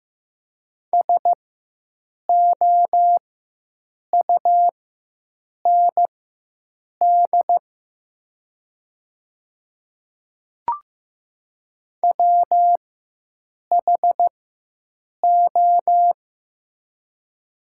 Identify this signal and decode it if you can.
Morse code, heard as a keyed tone, about 15 wpm, spaced as for 7 wpm, 700 Hz. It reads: SOUND WHO
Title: 4x Character Spacing - Top 100 Words - 15wpm